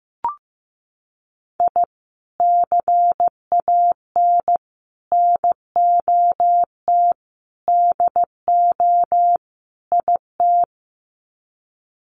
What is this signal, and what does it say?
Morse code, heard as a keyed tone, about 15 wpm, 700 Hz. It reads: I CAN NOT DO IT
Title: Sets of 5 Words Encore 15wpm